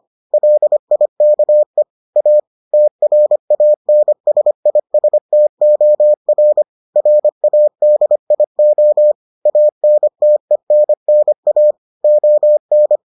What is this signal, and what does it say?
Morse code, heard as a keyed tone, about 25 wpm, 600 Hz. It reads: LIKE A TRANSISTOR RADIO ANTENNA ON